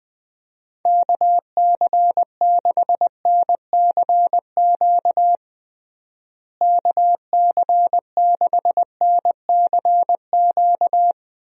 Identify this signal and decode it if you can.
Morse code, heard as a keyed tone, about 21 wpm, 700 Hz. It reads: KC6NCQ KC6NCQ
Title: USA Call Signs - 6 Characters - Repeated 2x - 20wpm